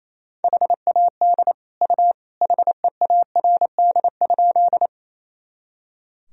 Morse code, heard as a keyed tone, about 28 wpm, 700 Hz. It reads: HAD U HEARD?
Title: Sentences from Top 400 Words and Abbreviations 28wpm